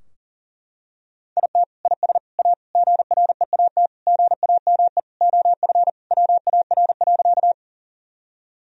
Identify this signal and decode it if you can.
Morse code, heard as a keyed tone, about 37 wpm, 700 Hz. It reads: IT IS A GREAT GAME OF WAR.
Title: Sentences from Top 1000 Words - Review Part 3 - 40wpm